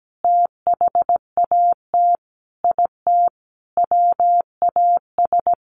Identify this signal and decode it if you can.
Morse code, heard as a keyed tone, about 17 wpm, 700 Hz. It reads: THAT IT WAS
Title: Sentences from Top 1500 Words 17wpm